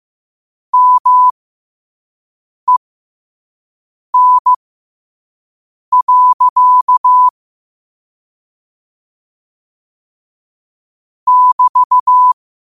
Morse code, heard as a keyed tone, about 15 wpm, spaced as for 5 wpm, 1000 Hz. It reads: MEN. <BT>
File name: qotdfiles / 5wpm